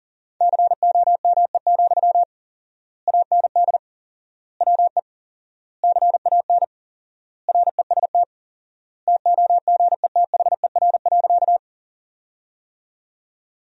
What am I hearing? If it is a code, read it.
Morse code, heard as a keyed tone, about 37 wpm, 700 Hz. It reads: COME, AND WE CAN REST TOGETHER.
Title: 4x Word-Spacing - Sentences from Top 500 Words - 40wpm